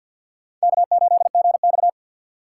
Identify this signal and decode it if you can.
Morse code, heard as a keyed tone, about 52 wpm, 700 Hz. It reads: K8GX
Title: USA Call Signs - 3x Character Spacing - 50wpm